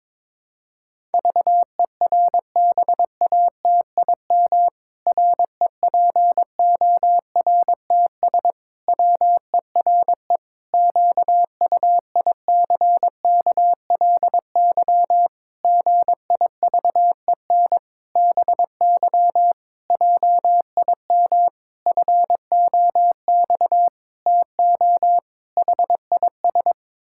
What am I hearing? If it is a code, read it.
Morse code, heard as a keyed tone, about 22 wpm, 700 Hz. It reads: VERBATIM REPORTS WERE QUICKLY GIVEN BY JIM FOX TO HIS
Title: Pangram Challenge v2 - 22wpm